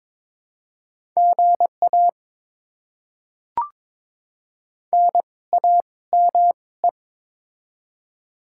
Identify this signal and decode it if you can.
Morse code, heard as a keyed tone, about 22 wpm, spaced as for 17 wpm, 700 Hz. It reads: GA NAME
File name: Top 20 QSO Elements - 2x Character Spacing - 22wpm